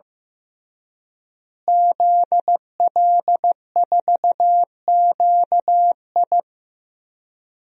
Morse code, heard as a keyed tone, about 15 wpm, 700 Hz. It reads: ZL4QI
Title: DXCC-100 Call Signs - 5 Characters - Repeated 3x - 15wpm